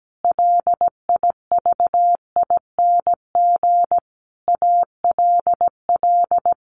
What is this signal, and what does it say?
Morse code, heard as a keyed tone, about 17 wpm, 700 Hz. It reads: LIVING ALL
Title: Sentences from Top 1500 Words 17wpm